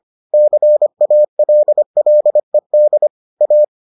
Morse code, heard as a keyed tone, about 25 wpm, 600 Hz. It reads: CALLED A